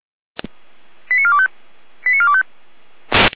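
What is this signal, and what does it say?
Five-tone selective call (ZVEI), heard as quick runs of five tones, tones of about 70 ms. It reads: 89425 89425
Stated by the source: ZVEI five-tone call received by SDR.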